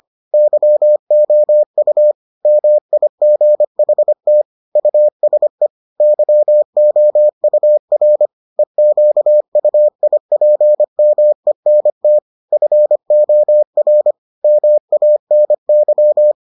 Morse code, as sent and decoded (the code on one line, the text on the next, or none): -.-- --- ..- / -- .. --. .... - / ..- ... / -.-- --- ..- .-. / . --.- ..- .. .--. -- . -. - / ..-. --- .-. / -- .- -. -.--
YOU MIGHT USE YOUR EQUIPMENT FOR MANY